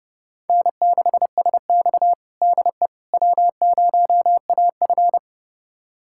Morse code, heard as a keyed tone, about 31 wpm, 700 Hz. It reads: N6SX DE W0AF
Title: USA Callsign DE Callsign - 4 Characters - 30wpm